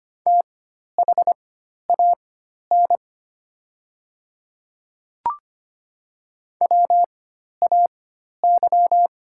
Morse code, heard as a keyed tone, about 25 wpm, spaced as for 12 wpm, 690 Hz. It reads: THAN WAY
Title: 4x Character Spacing - Top 100 Words - 25wpm